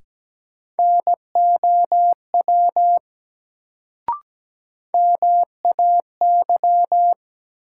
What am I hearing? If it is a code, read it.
Morse code, heard as a keyed tone, about 17 wpm, 700 Hz. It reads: NOW MAY
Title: Sets of 2 Words - New Word Drill - 17wpm